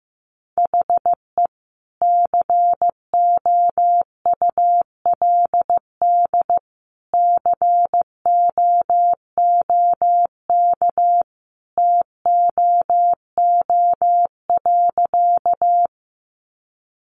Morse code, heard as a keyed tone, about 15 wpm, 700 Hz. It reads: HE COULD COOK TOO.
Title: Sentences from Top 700 Words 15wpm